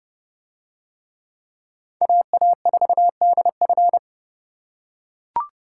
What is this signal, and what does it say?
Morse code, heard as a keyed tone, about 29 wpm, 700 Hz. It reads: AA4DF